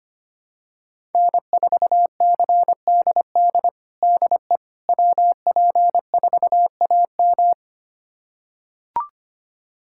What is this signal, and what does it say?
Morse code, heard as a keyed tone, about 25 wpm, 700 Hz. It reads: N4CDD DE WP4AM
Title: USA Callsign DE Callsign - 5 Characters - 25wpm